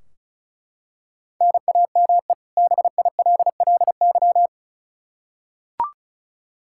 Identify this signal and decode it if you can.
Morse code, heard as a keyed tone, about 36 wpm, 700 Hz. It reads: NAME BILLY
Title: Practice QSO - Exchange 1 - Beginning - 35wpm